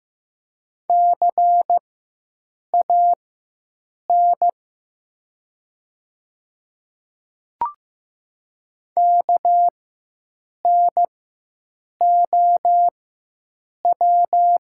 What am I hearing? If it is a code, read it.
Morse code, heard as a keyed tone, about 15 wpm, spaced as for 7 wpm, 700 Hz. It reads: CAN KNOW
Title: 4x Character Spacing - Top 100 Words - 15wpm